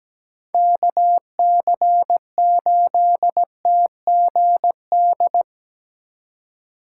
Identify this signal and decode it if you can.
Morse code, heard as a keyed tone, about 19 wpm, 700 Hz. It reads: KC8TGD
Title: Speed Racing - USA 2x3 Call Signs 17wpm